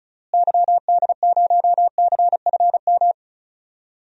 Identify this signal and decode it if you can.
Morse code, heard as a keyed tone, about 36 wpm, 700 Hz. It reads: YD0CFM
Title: DXCC-100 Call Signs - 6 Characters - Repeated 3x - 35wpm